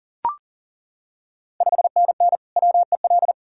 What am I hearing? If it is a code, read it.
Morse code, heard as a keyed tone, about 41 wpm, 690 Hz. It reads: 5NN WEL
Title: Ontario QSO Party Exchange 40wpm